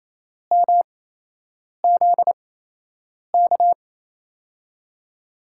Morse code, heard as a keyed tone, about 28 wpm, spaced as for 8 wpm, 700 Hz. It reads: MZK
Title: USA Call Signs - 5 Characters - 8x Character Spacing - 28wpm